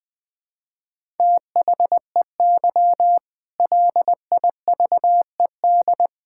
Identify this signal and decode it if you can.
Morse code, heard as a keyed tone, about 21 wpm, 700 Hz. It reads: THEY LIVED
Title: Sentences with 8-Word Phrases - Spoken Beginning - 20wpm